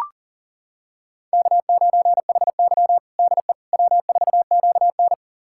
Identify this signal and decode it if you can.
Morse code, heard as a keyed tone, about 40 wpm, 700 Hz. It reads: K9HY DE W4QN
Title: USA Callsign DE Callsign - 4 Characters - 40wpm